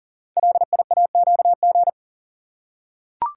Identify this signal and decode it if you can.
Morse code, heard as a keyed tone, about 39 wpm, 700 Hz. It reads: LIAQG
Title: Code Groups - 5 Letters - 40wpm